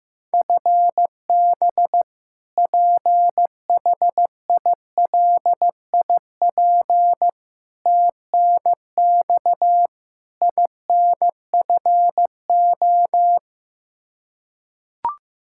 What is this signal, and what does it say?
Morse code, heard as a keyed tone, about 15 wpm, 700 Hz. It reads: FB PHILIP TNX INFO
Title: Practice QSO - Exchange 3 - More Info - 15wpm